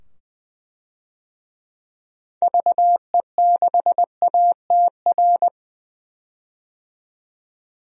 Morse code, heard as a keyed tone, about 21 wpm, 700 Hz. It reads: VE6ATR
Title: DXCC-100 Call Signs - 6 Characters - 20wpm